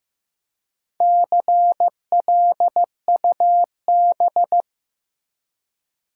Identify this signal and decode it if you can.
Morse code, heard as a keyed tone, about 15 wpm, 700 Hz. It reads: CLUB
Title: Four Letter Words - 15wpm